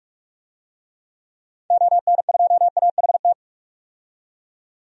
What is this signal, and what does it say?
Morse code, heard as a keyed tone, about 47 wpm, 690 Hz. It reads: ON2AHT
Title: DXCC-100 Call Signs - 6 Characters - 45wpm